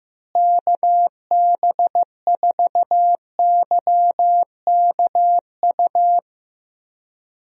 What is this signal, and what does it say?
Morse code, heard as a keyed tone, about 15 wpm, 700 Hz. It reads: KB4YKU
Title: USA Call Signs - 6 Characters - Repeated 2x - 15wpm